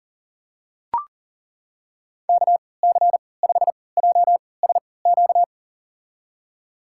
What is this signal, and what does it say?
Morse code, heard as a keyed tone, about 40 wpm, spaced as for 23 wpm, 700 Hz. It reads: KC5JSQ
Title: USA Call Signs - 3x Character Spacing - 40wpm